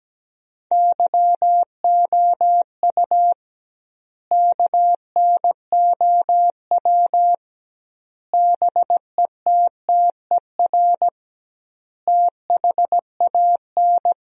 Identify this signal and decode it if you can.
Morse code, heard as a keyed tone, about 17 wpm, 700 Hz. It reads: YOU KNOW BETTER THAN
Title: 2x Word-Spacing - Sentences from Top 400 Words - 17wpm